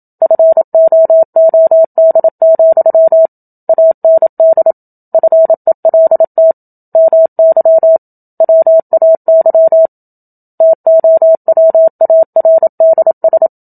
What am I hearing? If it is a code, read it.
Morse code, heard as a keyed tone, about 27 wpm, 650 Hz. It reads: FOOD, AND FELT MY WAY TOWARDS
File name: Book Two / 01-Chapter 1 - 27 WPM